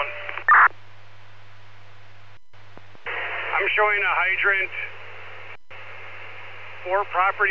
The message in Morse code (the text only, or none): none